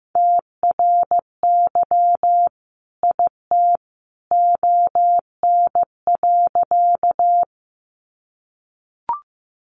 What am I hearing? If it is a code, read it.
Morse code, heard as a keyed tone, about 15 wpm, 700 Hz. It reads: TRY IT ON.
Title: Sentences from Top 200 Words 15wpm